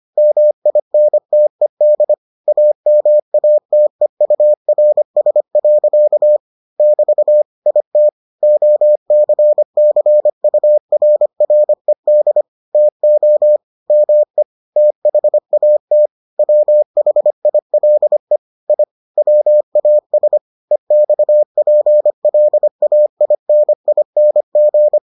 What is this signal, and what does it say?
Morse code, heard as a keyed tone, about 25 wpm, 600 Hz. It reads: MINTED AMATEURS. <BT> IT OCCURRED TO ME THAT WHILE I WAS EXPLAINING